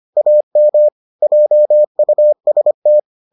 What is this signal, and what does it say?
Morse code, heard as a keyed tone, about 25 wpm, 600 Hz. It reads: AM JUST